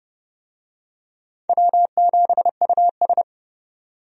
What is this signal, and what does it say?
Morse code, heard as a keyed tone, about 32 wpm, 700 Hz. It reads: W7US
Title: USA Call Signs - 4 Characters - 30wpm